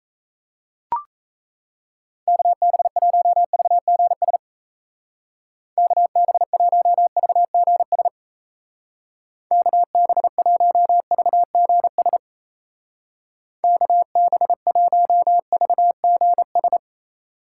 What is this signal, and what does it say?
Morse code, heard as a keyed tone, about 35 wpm, 700 Hz. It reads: KB1VGS KB1VGS KB1VGS KB1VGS
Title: Speed Racing - USA 2x3 Call Signs 28wpm